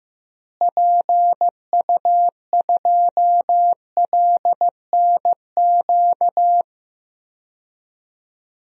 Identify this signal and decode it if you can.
Morse code, heard as a keyed tone, about 15 wpm, 700 Hz. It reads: PU2LNQ
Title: DXCC-100 Call Signs - 6 Characters - Repeated 2x - 15wpm